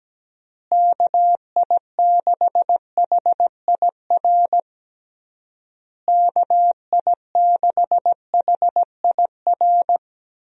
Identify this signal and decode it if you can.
Morse code, heard as a keyed tone, about 17 wpm, 700 Hz. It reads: KI6HIR KI6HIR
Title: USA Call Signs - 6 Characters - Repeated 3x - 17wpm